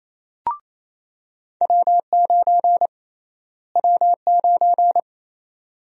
Morse code, heard as a keyed tone, about 28 wpm, 700 Hz. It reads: W9 W9
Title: USA Call Sign Prefixes - Repeated 2x - 28wpm